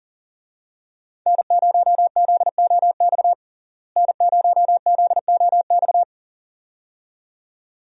Morse code, heard as a keyed tone, about 37 wpm, 700 Hz. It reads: N0ZOX N0ZOX
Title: USA Call Signs - 5 Characters - Repeated 2x - 40wpm